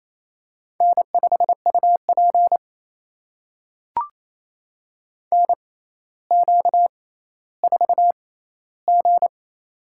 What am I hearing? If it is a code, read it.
Morse code, heard as a keyed tone, about 27 wpm, spaced as for 14 wpm, 700 Hz. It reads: N5UP NQ4G
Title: USA Call Signs - 4 Characters - 6x Character Spacing - 28wpm